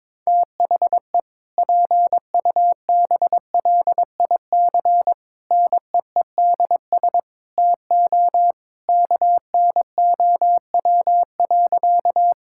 Morse code, heard as a keyed tone, about 22 wpm, 700 Hz. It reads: THE PUBLIC NEEDS TO KNOW.
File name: practice / 22wpm / Sentences with 2-Word Phrases - 22wpm